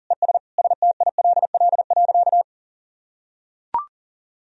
Morse code, heard as a keyed tone, about 40 wpm, 700 Hz. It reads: ES STILL.
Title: Sentences from Top 500 Words and Abbreviations 40wpm